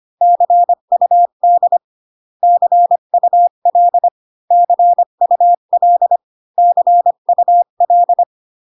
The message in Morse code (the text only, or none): CUD CUL CUL CUL